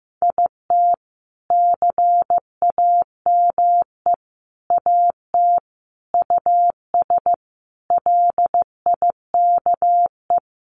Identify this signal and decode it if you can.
Morse code, heard as a keyed tone, about 15 wpm, 700 Hz. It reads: IT CAME AT US LIKE
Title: Sentences from Top 1400 Words 15wpm